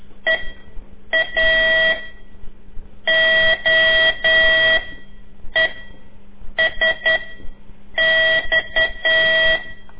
Morse code, heard as a keyed tone, about 10 wpm, spaced as for 7 wpm, 1950 Hz. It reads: EA O E S X
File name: CORSI CW